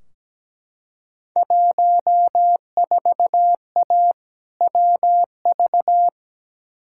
Morse code, heard as a keyed tone, about 17 wpm, 700 Hz. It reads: 14A WV
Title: Field Day Exchanges Encore - 17wpm